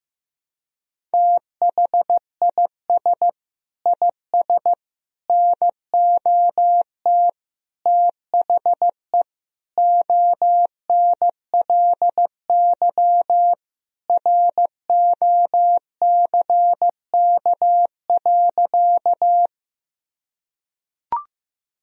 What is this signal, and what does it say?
Morse code, heard as a keyed tone, about 15 wpm, 700 Hz. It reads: THIS IS NOT THE ONLY ROCK.
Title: Sentences from Top 1000 Words - Review Part 2 - 15wpm